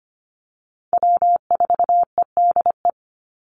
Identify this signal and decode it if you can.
Morse code, heard as a keyed tone, about 25 wpm, 700 Hz. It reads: W4EDE
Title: Call Signs Repeated 2x - 25wpm